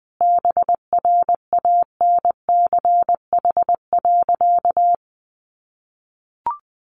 Morse code, heard as a keyed tone, about 20 wpm, 700 Hz. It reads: BRANCH.
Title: A Sound Of Thunder 20wpm